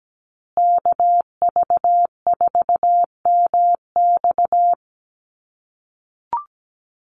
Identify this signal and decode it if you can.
Morse code, heard as a keyed tone, about 17 wpm, 700 Hz. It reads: KV4MX
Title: USA Call Signs Repeated 3x 17wpm